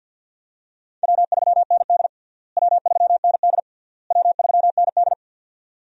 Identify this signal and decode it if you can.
Morse code, heard as a keyed tone, about 50 wpm, 700 Hz. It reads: W3ND W3ND W3ND
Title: USA Call Signs - 4 Characters - Repeated 3x - 50wpm